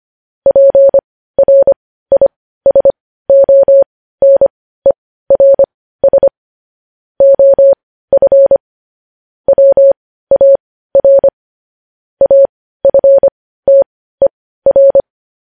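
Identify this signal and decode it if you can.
Morse code, heard as a keyed tone, about 25 wpm, spaced as for 15 wpm, 550 Hz. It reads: PRISONERS OF WAR AFTER
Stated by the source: Morse code news headlines.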